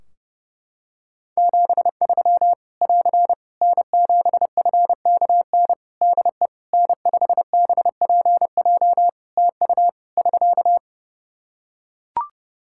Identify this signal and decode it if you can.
Morse code, heard as a keyed tone, about 30 wpm, 700 Hz. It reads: 73 <AR> N7FKN DE N5BPJ TU <SK>
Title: Practice QSO - Exchange 4 - Ending - 30wpm